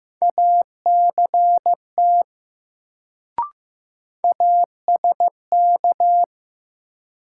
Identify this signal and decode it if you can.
Morse code, heard as a keyed tone, about 15 wpm, 700 Hz. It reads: ACT ASK
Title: Sentences from Top 200 Words - New Word Drill - 15wpm